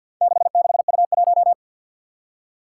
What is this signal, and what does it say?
Morse code, heard as a keyed tone, about 47 wpm, 700 Hz. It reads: 66U1
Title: Four Challenging Letter-Number Combinations 50wpm